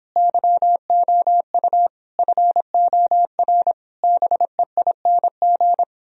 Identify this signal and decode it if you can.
Morse code, heard as a keyed tone, about 29 wpm, 700 Hz. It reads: YOU FOR BEING